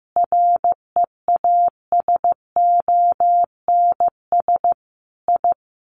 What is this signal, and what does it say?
Morse code, heard as a keyed tone, about 15 wpm, 700 Hz. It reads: REASONS I
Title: Sets of 5 Words Encore2 15wpm